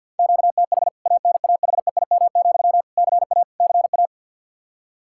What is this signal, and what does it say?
Morse code, heard as a keyed tone, about 52 wpm, 700 Hz. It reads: QTH ANAHEIM, CA CA
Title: K1USN SST QSO Practice 50wpm